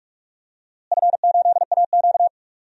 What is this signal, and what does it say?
Morse code, heard as a keyed tone, about 41 wpm, 700 Hz. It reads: F8AQ
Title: DXCC-100 Call Signs - 4 Characters - Repeated 2x - 45wpm